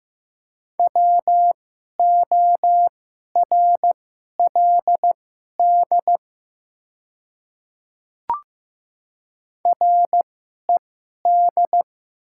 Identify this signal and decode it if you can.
Morse code, heard as a keyed tone, about 15 wpm, spaced as for 11 wpm, 700 Hz. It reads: WORLD RED
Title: 2x Character Spacing - Top 500 Words - 15wpm